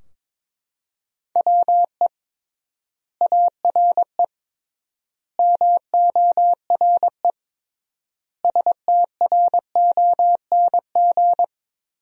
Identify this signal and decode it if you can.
Morse code, heard as a keyed tone, about 22 wpm, 700 Hz. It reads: WE ARE MORE STRONG
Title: 3x Word-Spacing - Sentences from Top 500 Words - 22wpm